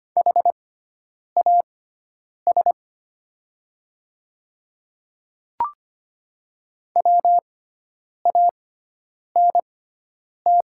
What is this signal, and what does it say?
Morse code, heard as a keyed tone, about 25 wpm, spaced as for 9 wpm, 690 Hz. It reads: HAS WANT
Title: 6x Character Spacing - Top 200 Words - 25wpm